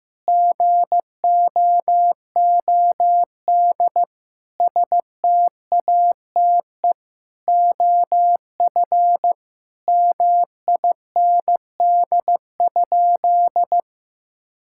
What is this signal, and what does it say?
Morse code, heard as a keyed tone, about 16 wpm, 700 Hz. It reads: GOOD STATE OF MIND?
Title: Sentences from Top 1000 Words - Review Part 5 - 15wpm